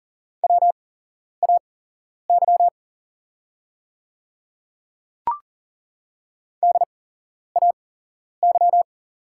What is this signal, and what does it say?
Morse code, heard as a keyed tone, about 40 wpm, spaced as for 11 wpm, 700 Hz. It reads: WAY DAY
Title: 8x Character Spacing - Top 100 Words - 40wpm